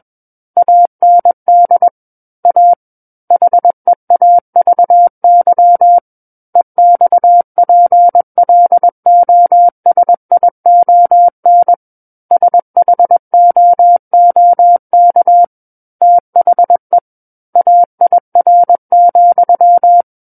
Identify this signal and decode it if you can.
Morse code, heard as a keyed tone, about 21 wpm, 700 Hz. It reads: AND A HEAVY EXPLOSION SHOOK THE AIR,
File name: Book One / 12-Chapter 12 - 21 WPM